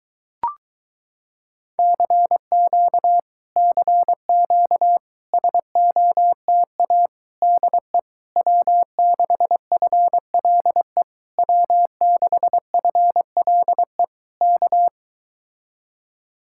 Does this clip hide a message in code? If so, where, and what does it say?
Yes: Morse code from 1.8 to 14.9 s, heard as a keyed tone, about 22 wpm, 700 Hz. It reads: CQ CQ SOTA DE W6FLE W6FLE K